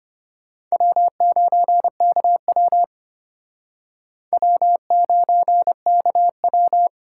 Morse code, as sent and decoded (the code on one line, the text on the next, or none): .-- ----. -.- .-- / .-- ----. -.- .--
W9KW W9KW